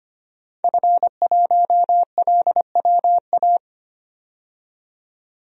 Morse code, heard as a keyed tone, about 25 wpm, 700 Hz. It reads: F1LWA
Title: DXCC-100 Call Signs - 5 Characters - 25wpm